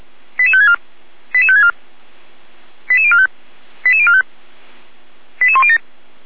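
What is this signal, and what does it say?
Five-tone selective call (ZVEI), heard as quick runs of five tones, tones of about 80 ms. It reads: 88564 88564 88045 88045 88107
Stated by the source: ZVEI five-tone selective call received by SDR, from feuerwehr-orlamuende.org fire brigade audio archive.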